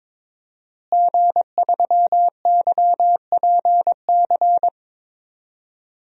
Morse code, heard as a keyed tone, about 22 wpm, 700 Hz. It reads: G3YPC
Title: DXCC-100 Call Signs - 5 Characters - Repeated 2x - 22wpm